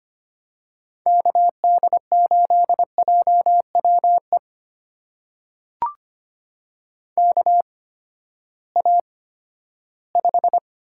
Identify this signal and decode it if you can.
Morse code, heard as a keyed tone, about 25 wpm, spaced as for 14 wpm, 700 Hz. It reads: KD8JWE KA5
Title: USA Call Signs - 8x Character Spacing - 25wpm